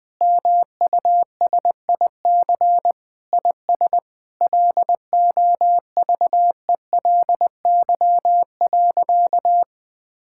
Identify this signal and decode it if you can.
Morse code, heard as a keyed tone, about 19 wpm, 700 Hz. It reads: MUSIC IS LOVELY.
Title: Sentences with 3-Word Phrases - 20wpm